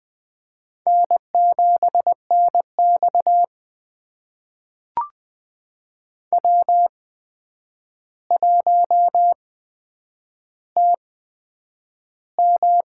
Morse code, heard as a keyed tone, about 20 wpm, spaced as for 9 wpm, 700 Hz. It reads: N7NX W1TM
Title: USA Call Signs - 4 Characters - 8x Character Spacing - 20wpm